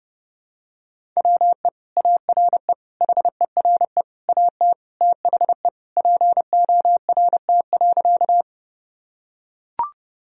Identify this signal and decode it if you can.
Morse code, heard as a keyed tone, about 30 wpm, 700 Hz. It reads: WE ARE HERE AT THE PORT.
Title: Sentences from Top 1000 Words - Review Part 1 - 30wpm